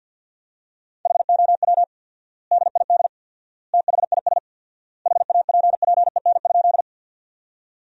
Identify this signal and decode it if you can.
Morse code, heard as a keyed tone, about 47 wpm, 700 Hz. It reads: HOW DID THIS HAPPEN?
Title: 4x Word-Spacing - Sentences from Top 400 Words - 50wpm